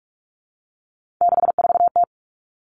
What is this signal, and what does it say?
Morse code, heard as a keyed tone, about 44 wpm, 700 Hz. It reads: 64T
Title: Common Cut Numbers - 3-Digit 45wpm